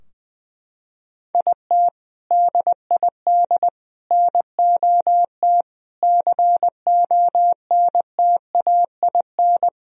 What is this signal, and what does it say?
Morse code, heard as a keyed tone, about 21 wpm, 700 Hz. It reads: IT DID NOT CONTAIN
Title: Sentences from Top 1000 Words - Review Part 2 - 20wpm